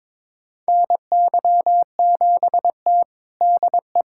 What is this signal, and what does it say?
Morse code, heard as a keyed tone, about 21 wpm, 700 Hz. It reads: NY7T DE
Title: USA Callsign DE Callsign - 4 Characters - 22wpm